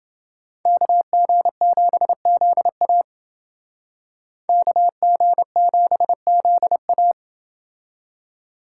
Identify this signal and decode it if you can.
Morse code, heard as a keyed tone, about 29 wpm, 700 Hz. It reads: KG7ZA KG7ZA